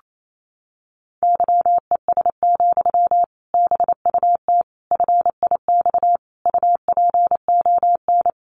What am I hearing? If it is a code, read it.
Morse code, heard as a keyed tone, about 28 wpm, 700 Hz. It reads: YES, BUT FIX UPON
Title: Sentences from Top 1400 Words 28wpm